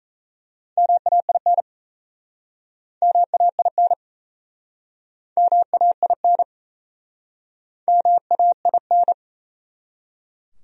Morse code, heard as a keyed tone, about 38 wpm, 700 Hz. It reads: MAIN MAIN MAIN MAIN